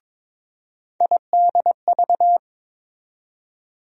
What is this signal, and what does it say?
Morse code, heard as a keyed tone, about 23 wpm, 700 Hz. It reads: IDV
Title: USA Call Sign Postfixes - 3 Letters 22wpm